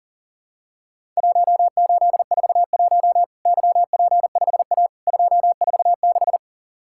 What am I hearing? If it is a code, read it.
Morse code, heard as a keyed tone, about 40 wpm, 700 Hz. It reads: 1841 YP5A 246
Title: Worked All Europe DX Contest - QTC Exchange - 40wpm